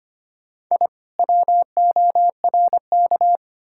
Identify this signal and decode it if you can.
Morse code, heard as a keyed tone, about 33 wpm, 700 Hz. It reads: I WORK